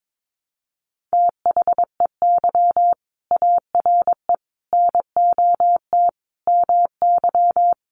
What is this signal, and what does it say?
Morse code, heard as a keyed tone, about 22 wpm, 700 Hz. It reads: THEY ARE NOT MY
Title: Sentences from Top 1100 Words 22wpm